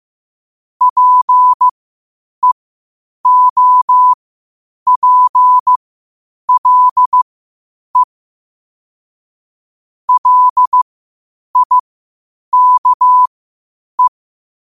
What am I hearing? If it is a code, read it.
Morse code, heard as a keyed tone, about 15 wpm, spaced as for 8 wpm, 1000 Hz. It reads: PEOPLE LIKE
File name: 8wpm